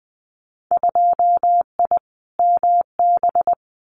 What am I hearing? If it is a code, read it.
Morse code, heard as a keyed tone, about 20 wpm, 700 Hz. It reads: 2I MB
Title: Winter Field Day Exchanges Encore 20wpm